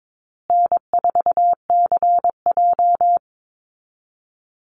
Morse code, heard as a keyed tone, about 22 wpm, 700 Hz. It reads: N4CJ
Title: Easy Call Signs Repeated 3x - 22wpm